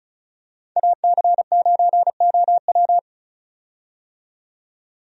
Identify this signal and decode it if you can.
Morse code, heard as a keyed tone, about 36 wpm, 700 Hz. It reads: AC9OW
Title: USA Call Signs - 5 Characters - 35wpm